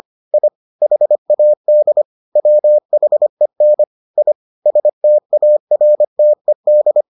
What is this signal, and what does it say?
Morse code, heard as a keyed tone, about 25 wpm, 600 Hz. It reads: I HAD WHEN I STARTED